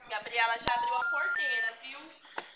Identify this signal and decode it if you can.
Special information tone, with speech.